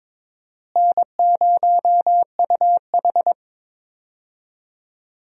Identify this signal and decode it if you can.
Morse code, heard as a keyed tone, about 22 wpm, 700 Hz. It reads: N0UH
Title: USA Call Signs - 4 Characters - Repeated 2x - 22wpm